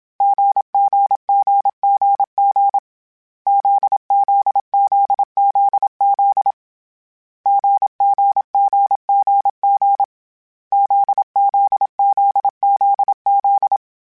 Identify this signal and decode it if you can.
Morse code, heard as a keyed tone, about 27 wpm, 800 Hz. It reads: GGGGG ZZZZZ GGGGG ZZZZZ